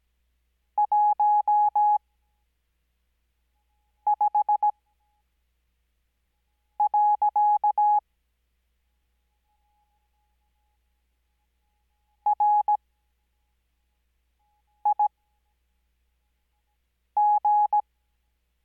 Morse code, heard as a keyed tone, about 17 wpm, spaced as for 4 wpm, 840 Hz. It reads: 15. RIG
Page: Farnsworth Spacing: 15WPM Character speed Spaced at 5WPM.